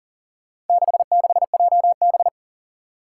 Morse code, heard as a keyed tone, about 40 wpm, 700 Hz. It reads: 66JB
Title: Four Challenging Letter-Number Combinations 40wpm